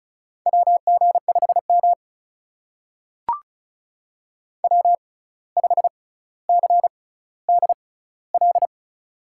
Morse code, heard as a keyed tone, about 36 wpm, spaced as for 17 wpm, 700 Hz. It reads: WG5M W5CDL
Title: USA Call Signs - 6x Character Spacing - 35wpm